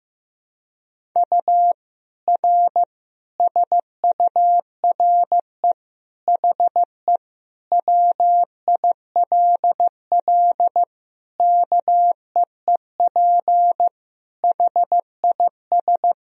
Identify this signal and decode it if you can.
Morse code, heard as a keyed tone, about 16 wpm, 700 Hz. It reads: U R SURE HE WILL KEEP HIS
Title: Sentences from Top 300 Words and Abbreviations 15wpm